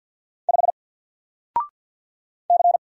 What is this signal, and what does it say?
Morse code, heard as a keyed tone, about 46 wpm, 700 Hz. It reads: 5 C
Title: Single Character (Letter:Number:Punctuation:Prosign) 50wpm